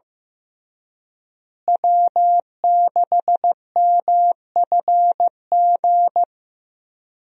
Morse code, heard as a keyed tone, about 15 wpm, 700 Hz. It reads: W6MFG
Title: USA Call Signs - 5 Characters - Repeated 3x - 15wpm